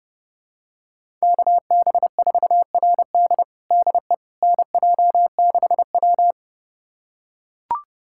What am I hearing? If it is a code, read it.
Morse code, heard as a keyed tone, about 30 wpm, 700 Hz. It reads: KB4RD DE NJ6W
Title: USA Callsign DE Callsign 30wpm